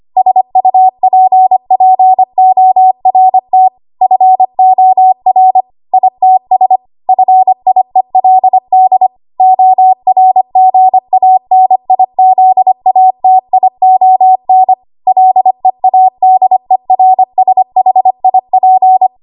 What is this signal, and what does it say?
Morse code, heard as a keyed tone, about 25 wpm, 750 Hz. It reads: SUPPORT FOR ITS FIELD ORGANIZATION LEADERSHIP
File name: Archive / 25 WPM / 25WPM